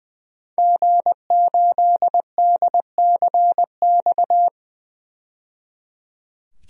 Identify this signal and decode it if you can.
Morse code, heard as a keyed tone, about 20 wpm, 700 Hz. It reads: G8DCX
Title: DXCC-100 Call Signs - 5 Characters - 20wpm